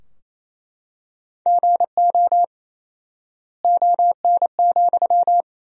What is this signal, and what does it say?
Morse code, heard as a keyed tone, about 28 wpm, 700 Hz. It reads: GO ON,
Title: 4x Word-Spacing - Sentences from Top 100 Words - 28wpm